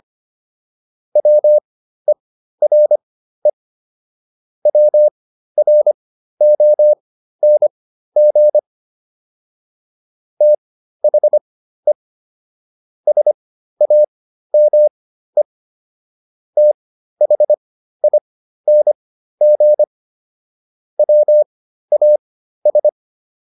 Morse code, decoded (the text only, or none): WERE WRONG THE SAME THING WAS